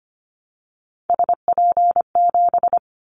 Morse code, heard as a keyed tone, about 25 wpm, 700 Hz. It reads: SP7
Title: Callsign Prefixes for Top 25 DXCC - Repeated 3x - 25wpm